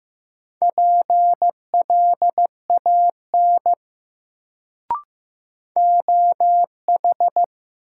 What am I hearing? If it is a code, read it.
Morse code, heard as a keyed tone, about 16 wpm, 700 Hz. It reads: PLAN OH
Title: Sentences from Top 500 Words - New Word Drill - 15wpm